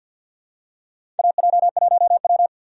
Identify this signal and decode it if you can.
Morse code, heard as a keyed tone, about 49 wpm, 700 Hz. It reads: AJ1W
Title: USA Call Signs - 4 Characters - 3x Character Spacing - 50wpm